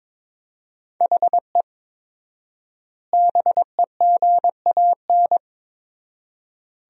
Morse code, heard as a keyed tone, about 22 wpm, 700 Hz. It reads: HE BEGAN
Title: 4x Word-Spacing - Sentences from Top 500 Words - 22wpm